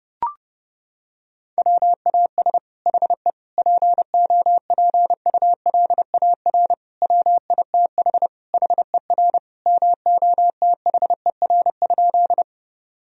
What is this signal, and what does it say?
Morse code, heard as a keyed tone, about 32 wpm, 700 Hz. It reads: WAS HE POPULAR WITH HER MOTHER?
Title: Sentences from Top 1000 Words - Review Part 3 - 30wpm